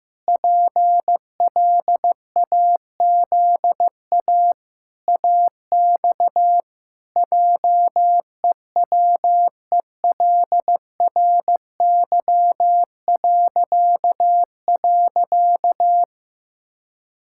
Morse code, heard as a keyed tone, about 15 wpm, 700 Hz. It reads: PLAZA AX JEWELRY..
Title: Pangram Challenge v2 - 15wpm